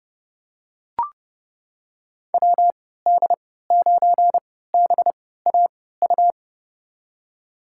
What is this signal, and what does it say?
Morse code, heard as a keyed tone, about 30 wpm, spaced as for 17 wpm, 700 Hz. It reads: WD9BAU
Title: USA Call Signs - 6 Characters - 3x Character Spacing - 30wpm